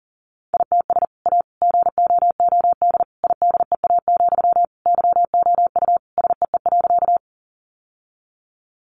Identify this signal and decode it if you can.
Morse code, heard as a keyed tone, about 40 wpm, 700 Hz. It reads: ITS A GOOD IDEA, YOU SEE.
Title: Sentences from Top 600 Words 40wpm